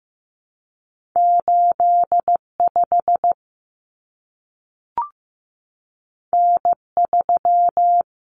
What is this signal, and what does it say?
Morse code, heard as a keyed tone, about 15 wpm, 700 Hz. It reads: 85 N3
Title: Common Cut Numbers - 2-Digit 15wpm